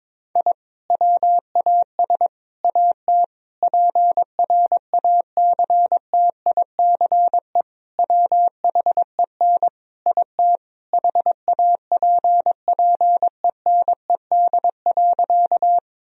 Morse code, heard as a keyed tone, about 22 wpm, 700 Hz. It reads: I WAS AT PRACTICE WHEN IT HAPPENED.
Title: Sentences from Top 1000 Words - Review Part 4 - 22wpm